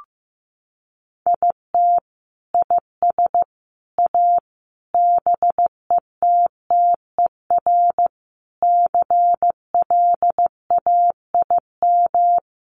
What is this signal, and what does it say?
Morse code, heard as a keyed tone, about 15 wpm, 700 Hz. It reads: IT IS A BETTER CLAIM
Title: Sentences from Top 1500 Words 15wpm